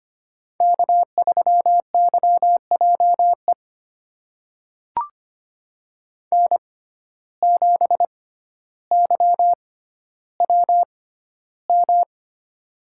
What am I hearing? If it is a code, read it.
Morse code, heard as a keyed tone, about 25 wpm, spaced as for 13 wpm, 700 Hz. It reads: K3YJE N7YWM
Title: USA Call Signs - 5 Characters - 6x Character Spacing - 25wpm